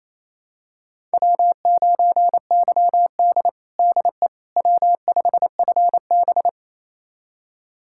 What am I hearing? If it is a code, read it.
Morse code, heard as a keyed tone, about 28 wpm, 700 Hz. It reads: W9YD DE W5FB
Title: USA Callsign DE Callsign - 4 Characters - 28wpm